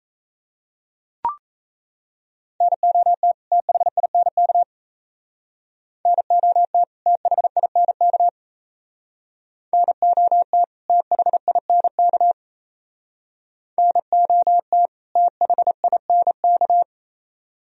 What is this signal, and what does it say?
Morse code, heard as a keyed tone, about 36 wpm, 700 Hz. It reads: NOT THINK NOT THINK NOT THINK NOT THINK